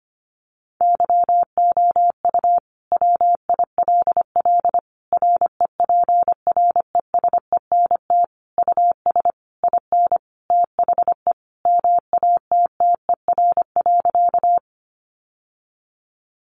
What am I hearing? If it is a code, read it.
Morse code, heard as a keyed tone, about 25 wpm, 700 Hz. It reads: YOU WILL REPRESENT US IN THE MATTER.
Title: Sentences from Top 1200 Words 25wpm